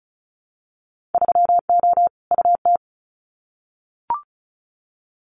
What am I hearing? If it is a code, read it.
Morse code, heard as a keyed tone, about 36 wpm, 700 Hz. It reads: 3O UT
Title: Winter Field Day Exchanges 35wpm